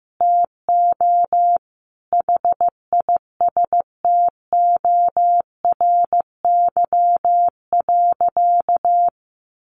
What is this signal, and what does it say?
Morse code, heard as a keyed tone, about 15 wpm, 700 Hz. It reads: TO HISTORY.
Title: A Sound Of Thunder 15wpm